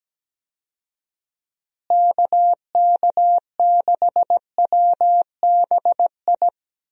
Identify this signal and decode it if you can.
Morse code, heard as a keyed tone, about 17 wpm, 700 Hz. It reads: KK6WBI
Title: USA Call Signs - 6 Characters - Repeated 3x - 17wpm